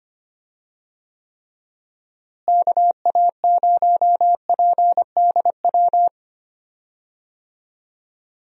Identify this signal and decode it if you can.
Morse code, heard as a keyed tone, about 25 wpm, 700 Hz. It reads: KA0PDW
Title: USA Call Signs - 6 Characters - Repeated 3x - 25wpm